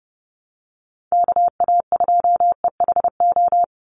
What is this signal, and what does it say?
Morse code, heard as a keyed tone, about 30 wpm, 700 Hz. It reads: KA2EHO
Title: USA Call Signs 30wpm